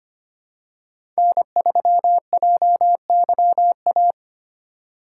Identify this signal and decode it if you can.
Morse code, heard as a keyed tone, about 25 wpm, 700 Hz. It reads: N3JYA
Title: USA Call Signs - 5 Characters - 8x Character Spacing - 25wpm